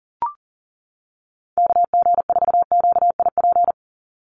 Morse code, heard as a keyed tone, about 40 wpm, 700 Hz. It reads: KG4QIP
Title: USA Call Signs 40wpm